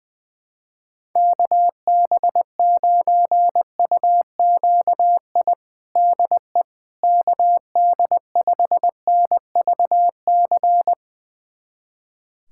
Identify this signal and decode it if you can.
Morse code, heard as a keyed tone, about 20 wpm, 700 Hz. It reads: KB9UQI DE KD5NVC
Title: USA Callsign DE Callsign - 6 Characters - 20wpm